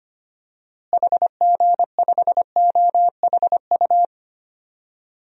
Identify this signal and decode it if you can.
Morse code, heard as a keyed tone, about 25 wpm, 690 Hz. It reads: HG5OHU
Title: DXCC-100 Call Signs - 6 Characters - Repeated 3x - 25wpm